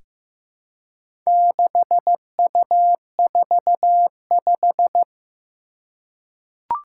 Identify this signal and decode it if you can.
Morse code, heard as a keyed tone, about 15 wpm, 700 Hz. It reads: 6U45
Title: Four Challenging Letter-Number Combinations 15wpm